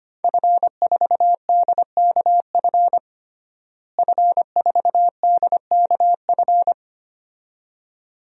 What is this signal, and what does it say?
Morse code, heard as a keyed tone, about 25 wpm, 700 Hz. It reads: F4DKF F4DKF
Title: DXCC-100 Call Signs - 5 Characters - Repeated 2x - 25wpm